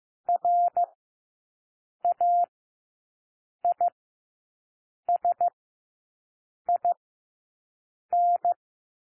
Morse code, heard as a keyed tone, about 15 wpm, spaced as for 6 wpm, 700 Hz. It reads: RAISIN
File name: cwmp3 / W1 SOUND WORDS 15 wpm